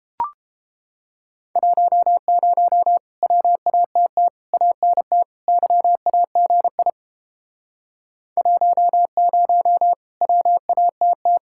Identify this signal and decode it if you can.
Morse code, heard as a keyed tone, about 31 wpm, 700 Hz. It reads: 10 WATT ANT YAGI 10 WATT